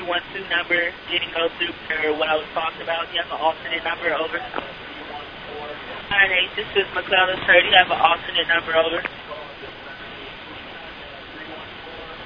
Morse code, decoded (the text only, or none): none